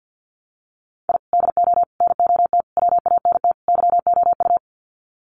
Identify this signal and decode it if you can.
Morse code, heard as a keyed tone, about 50 wpm, 700 Hz. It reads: I DO NOT WANT YOU
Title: Sets of 5 Words 50wpm